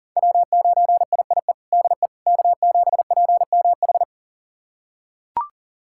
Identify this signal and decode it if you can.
Morse code, heard as a keyed tone, about 37 wpm, 700 Hz. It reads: W9IIE DE K7PMH
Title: USA Callsign DE Callsign - 5 Characters - 40wpm